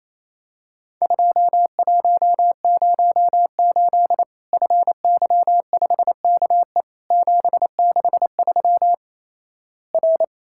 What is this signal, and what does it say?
Morse code, heard as a keyed tone, about 28 wpm, 700 Hz. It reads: 2108 FY5KE 763 R
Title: Worked All Europe DX Contest - QTC Exchange - 28wpm